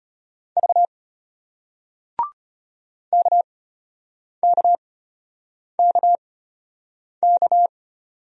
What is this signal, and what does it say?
Morse code, heard as a keyed tone, about 33 wpm, spaced as for 8 wpm, 700 Hz. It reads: V KKKK